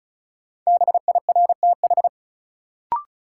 Morse code, heard as a keyed tone, about 36 wpm, 700 Hz. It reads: BIRTH
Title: Sentences from Top 800 Words - New Word Drill - 35wpm